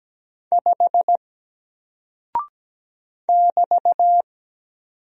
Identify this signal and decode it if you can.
Morse code, heard as a keyed tone, about 17 wpm, 700 Hz. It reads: 5 <BT>
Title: Single Character (Letter:Number:Punctuation:Prosign) Alt 17wpm